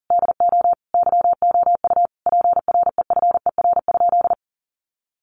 Morse code, heard as a keyed tone, about 40 wpm, 700 Hz. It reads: DO YOU PREFER?
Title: Sentences from Top 1100 Words 40wpm